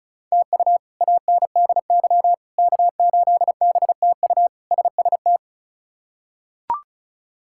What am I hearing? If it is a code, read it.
Morse code, heard as a keyed tone, about 36 wpm, 700 Hz. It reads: TU ANDY K8BTU SST
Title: K1USN SST QSO Practice - 35wpm